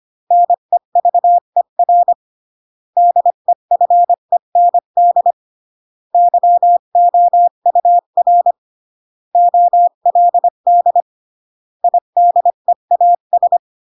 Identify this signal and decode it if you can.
Morse code, heard as a keyed tone, about 25 wpm, 700 Hz. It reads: NEVER DEFEND YOUR OLD IDEAS